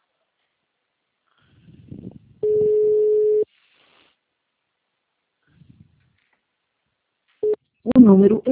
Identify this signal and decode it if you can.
Ringback tone (Europe, 425 Hz), with speech.